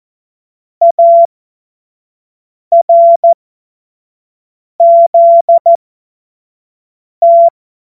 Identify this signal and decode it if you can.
Morse code, heard as a keyed tone, about 14 wpm, spaced as for 5 wpm, 680 Hz. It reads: ARZT
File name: Morsekurs / Witze / Witz 02 April 2024 25 F70